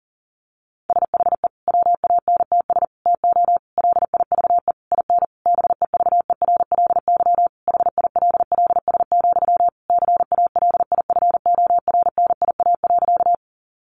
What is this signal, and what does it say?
Morse code, heard as a keyed tone, about 37 wpm, 700 Hz. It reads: SHE WANTS TO LIVE IN BEVERLY HILLS, CALIFORNIA.
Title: Easy Phrases 40wpm